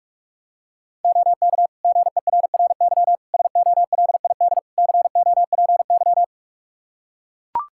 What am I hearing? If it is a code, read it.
Morse code, heard as a keyed tone, about 41 wpm, 700 Hz. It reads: OK GERRY SOLID COPY
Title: Practice QSO - Exchange 3 - More Info - 45wpm